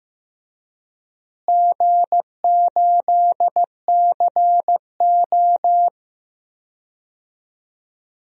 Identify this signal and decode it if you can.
Morse code, heard as a keyed tone, about 15 wpm, 700 Hz. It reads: G8CO
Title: DXCC-100 Call Signs - 4 Characters - Repeated 2x - 15wpm